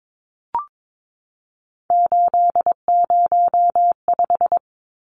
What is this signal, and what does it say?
Morse code, heard as a keyed tone, about 22 wpm, 700 Hz. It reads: 805
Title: Numbers - 3-Digit 22wpm